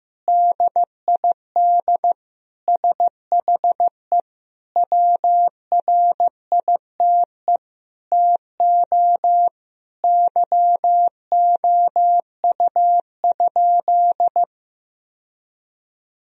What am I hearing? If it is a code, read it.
Morse code, heard as a keyed tone, about 15 wpm, 700 Hz. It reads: DID SHE WRITE TO YOU?
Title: Sentences from Top 1000 Words - Review Part 1 - 15wpm